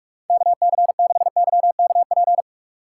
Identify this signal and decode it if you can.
Morse code, heard as a keyed tone, about 46 wpm, 700 Hz. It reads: KC6YKP
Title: USA Call Signs - 6 Characters - 3x Character Spacing - 45wpm